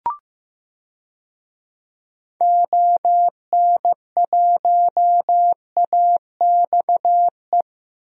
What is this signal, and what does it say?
Morse code, heard as a keyed tone, about 15 wpm, 700 Hz. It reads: ON1AXE
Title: DXCC-100 Call Signs - 6 Characters - 15wpm